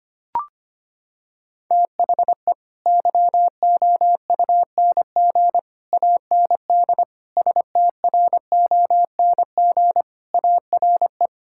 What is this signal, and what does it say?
Morse code, heard as a keyed tone, about 25 wpm, 700 Hz. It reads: THE YOUNG AND STRONG ARE
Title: Sentences from Top 1000 Words - Review Part 5 - 25wpm